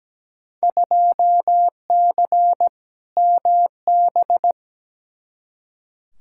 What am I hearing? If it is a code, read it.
Morse code, heard as a keyed tone, about 17 wpm, 700 Hz. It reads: 2C MB
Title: Field Day Exchanges Encore - 17wpm